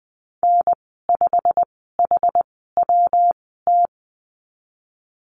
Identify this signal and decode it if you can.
Morse code, heard as a keyed tone, about 20 wpm, 700 Hz. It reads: N 5 H W T
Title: 154 Lesson 25g - Letter p - Call Signs 20wpm